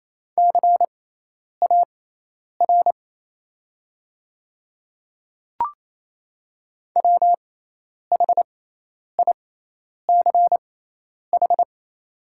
Morse code, heard as a keyed tone, about 28 wpm, spaced as for 10 wpm, 700 Hz. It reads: CAR WHICH
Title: 6x Character Spacing - Top 300 Words - 28wpm